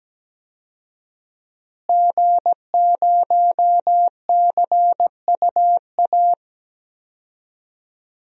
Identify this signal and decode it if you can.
Morse code, heard as a keyed tone, about 17 wpm, 700 Hz. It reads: G0CUA
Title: DXCC-100 Call Signs - 5 Characters - Repeated 3x - 17wpm